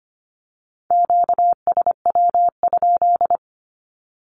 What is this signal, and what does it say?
Morse code, heard as a keyed tone, about 25 wpm, 700 Hz. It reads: QSW?
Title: Q-Codes with Spoken Meaning - 25wpm